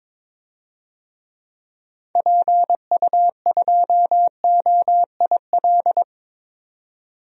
Morse code, heard as a keyed tone, about 22 wpm, 700 Hz. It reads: PU2OIL
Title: DXCC-100 Call Signs - 6 Characters - 22wpm